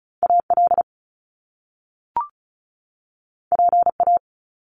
Morse code, heard as a keyed tone, about 36 wpm, 700 Hz. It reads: AL PA
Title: Top Two Letter Combinations in Words 35wpm